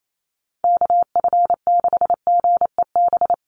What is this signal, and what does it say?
Morse code, heard as a keyed tone, about 28 wpm, 700 Hz. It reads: KF6GEB
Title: USA Call Signs Repeated 3x 28wpm